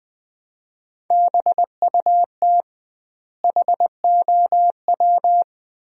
Morse code, heard as a keyed tone, about 20 wpm, 700 Hz. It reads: BUT HOW
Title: 2x Word-Spacing - Sentences from Top 100 Words - 20wpm